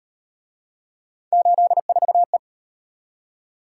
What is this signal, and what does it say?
Morse code, heard as a keyed tone, about 36 wpm, 700 Hz. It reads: 84E